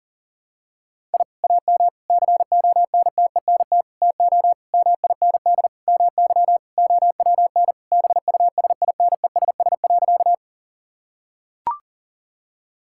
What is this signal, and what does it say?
Morse code, heard as a keyed tone, about 40 wpm, 700 Hz. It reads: I AM CONTENT TO MIND MY OWN BUSINESS.
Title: Sentences from Top 1000 Words - Review Part 6 - 40wpm